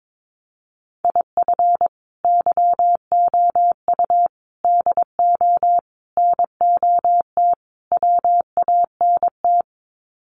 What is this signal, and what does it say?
Morse code, heard as a keyed tone, about 22 wpm, 700 Hz. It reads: IF YOU DO NOT WANT
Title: Sets of 5 Words 22wpm